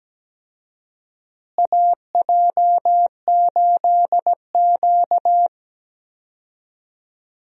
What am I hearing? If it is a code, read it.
Morse code, heard as a keyed tone, about 17 wpm, 700 Hz. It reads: AJ8Q
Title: USA Call Signs - 4 Characters - Repeated 3x - 17wpm